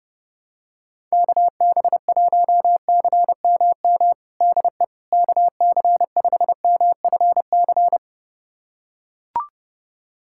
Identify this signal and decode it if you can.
Morse code, heard as a keyed tone, about 32 wpm, 700 Hz. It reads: KB1CMM DE KC5MFC
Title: USA Callsign DE Callsign - 6 Characters - 30wpm